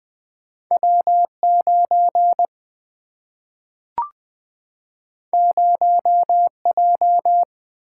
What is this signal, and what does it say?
Morse code, heard as a keyed tone, about 20 wpm, 700 Hz. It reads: W9 0J
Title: Single-Letter Single-Number 20wpm